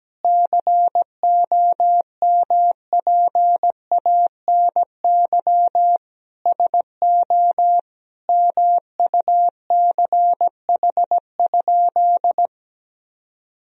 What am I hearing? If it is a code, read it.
Morse code, heard as a keyed tone, about 17 wpm, 700 Hz. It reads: COMPANY SO MUCH?
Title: Sentences from Top 1000 Words - Review Part 3 - 17wpm